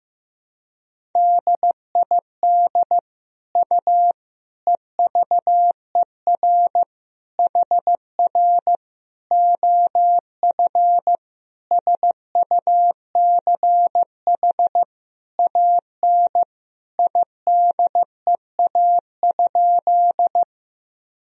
Morse code, heard as a keyed tone, about 15 wpm, 700 Hz. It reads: DID U EVER HR OF SUCH AN IDEA?
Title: Sentences from Top 400 Words and Abbreviations 15wpm